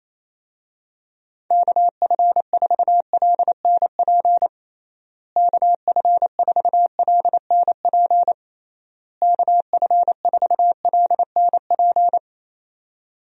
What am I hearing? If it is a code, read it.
Morse code, heard as a keyed tone, about 27 wpm, 700 Hz. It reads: KF4LNP KF4LNP KF4LNP
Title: USA Call Signs - 6 Characters - Repeated 3x - 28wpm